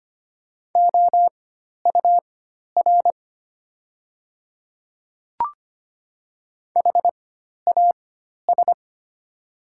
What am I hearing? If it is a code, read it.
Morse code, heard as a keyed tone, about 25 wpm, spaced as for 12 wpm, 700 Hz. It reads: OUR HAS
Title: 4x Character Spacing - Top 300 Words - 25wpm